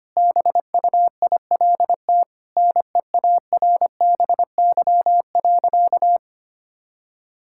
Morse code, heard as a keyed tone, about 25 wpm, 700 Hz. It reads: BUILT NEARBY.
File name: practice / 25wpm / Sentences with 2-Word Phrases - 25wpm